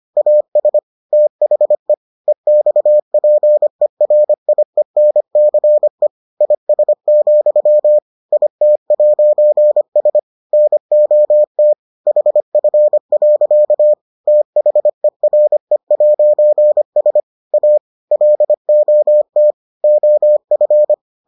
Morse code, heard as a keyed tone, about 25 wpm, 600 Hz. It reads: AS THE EXPERIENCE IS, IT'S NOT HF. THERE'S A LOT OF